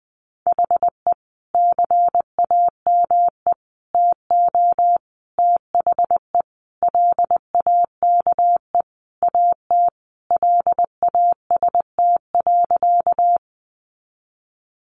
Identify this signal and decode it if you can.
Morse code, heard as a keyed tone, about 20 wpm, 700 Hz. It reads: HE CAME TO THE LAKE AT LAST.
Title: Sentences from Top 1100 Words 20wpm